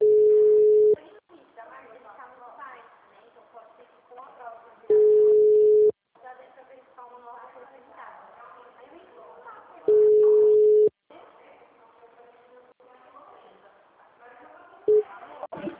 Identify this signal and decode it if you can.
Ringback tone (Europe, 425 Hz), with speech.
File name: out- -2012- - - .28441